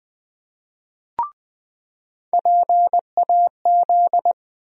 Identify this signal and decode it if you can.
Morse code, heard as a keyed tone, about 20 wpm, 700 Hz. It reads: PAZ
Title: USA Call Sign Postfixes - 3 Letters 20wpm